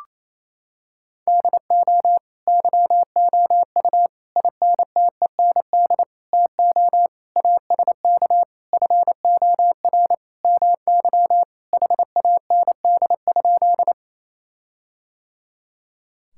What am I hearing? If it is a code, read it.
Morse code, heard as a keyed tone, about 28 wpm, 700 Hz. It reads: DO YOU INTEND TO ASK FOR MY HAND?
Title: Sentences from Top 1600 Words 28wpm